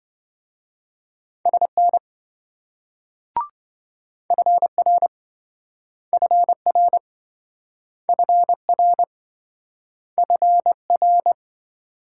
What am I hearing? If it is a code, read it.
Morse code, heard as a keyed tone, about 28 wpm, 700 Hz. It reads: SN FR FR FR FR